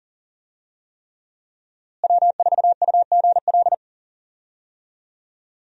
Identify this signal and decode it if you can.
Morse code, heard as a keyed tone, about 40 wpm, 700 Hz. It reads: W4UGL
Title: USA Call Signs - 5 Characters - Repeated 3x - 40wpm